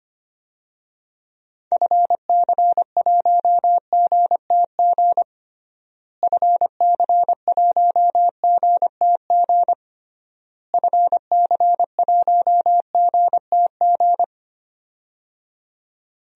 Morse code, heard as a keyed tone, about 25 wpm, 700 Hz. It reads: FC1GTG FC1GTG FC1GTG
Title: DXCC-100 Call Signs - 6 Characters - Repeated 3x - 25wpm